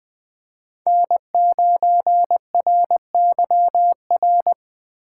Morse code, heard as a keyed tone, about 20 wpm, 700 Hz. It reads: N9RYR
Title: USA Call Signs - 5 Characters - 2x Character Spacing - 20wpm